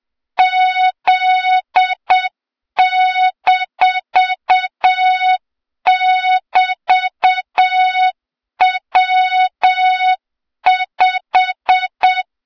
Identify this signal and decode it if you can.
Morse code, heard as a keyed tone, about 7 wpm, 3700 Hz, with noise. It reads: Z-<BT>W5
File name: Morze-HA2MN / hang / 23 GYAKORLAT